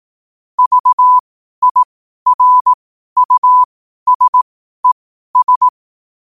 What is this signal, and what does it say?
Morse code, heard as a keyed tone, about 18 wpm, spaced as for 12 wpm, 1000 Hz. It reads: VIRUSES
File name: qotdfiles / 12wpm